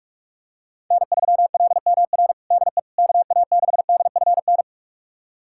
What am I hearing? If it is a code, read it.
Morse code, heard as a keyed tone, about 45 wpm, 700 Hz. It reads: N3LMR DE KA6DFN
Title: USA Callsign DE Callsign 45wpm